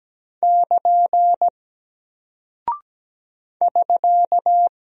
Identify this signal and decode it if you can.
Morse code, heard as a keyed tone, about 17 wpm, 700 Hz. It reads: <KN> <SK>
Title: Single Character (Letter:Number:Punctuation:Prosign) Alt 17wpm